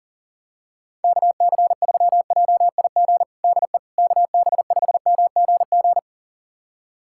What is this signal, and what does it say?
Morse code, heard as a keyed tone, about 40 wpm, 700 Hz. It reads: KC3JIG DE KB5MGG
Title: USA Callsign DE Callsign 40wpm